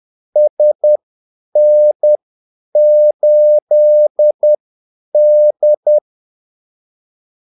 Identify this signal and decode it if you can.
Morse code, heard as a keyed tone, about 10 wpm, spaced as for 8 wpm, 600 Hz. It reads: SN8D
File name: Callsigns-10-8wpm